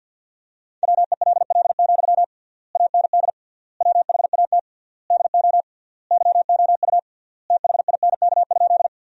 Morse code, heard as a keyed tone, about 53 wpm, 700 Hz. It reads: WELL, AND WHAT DO YOU THINK?
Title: 3x Word-Spacing - Sentences from Top 200 Words - 50wpm